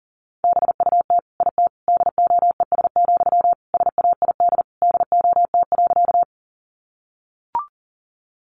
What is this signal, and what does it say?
Morse code, heard as a keyed tone, about 40 wpm, 700 Hz. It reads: BUT IT DOES, SAID DOT.
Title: Sentences from Top 1500 Words 40wpm